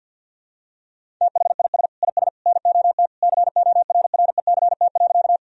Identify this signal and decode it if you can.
Morse code, heard as a keyed tone, about 46 wpm, 700 Hz. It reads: THIS IS NOT CORRECT.
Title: Sentences from Top 1000 Words - Review Part 2 - 50wpm